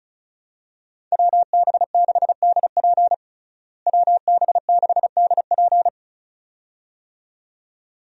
Morse code, heard as a keyed tone, about 34 wpm, 700 Hz. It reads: WB6DP WB6DP
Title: USA Call Signs - 5 Characters - Repeated 2x - 35wpm